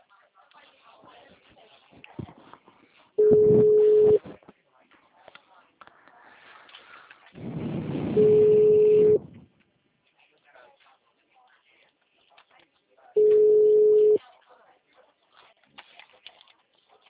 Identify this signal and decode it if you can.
Ringback tone (Europe, 425 Hz), with speech.